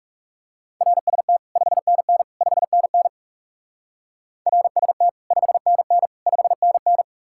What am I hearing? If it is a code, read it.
Morse code, heard as a keyed tone, about 36 wpm, 700 Hz. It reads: RST 5NN 5NN RST 5NN 5NN
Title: Speed-Racing Common QSO Phrases - 30wpm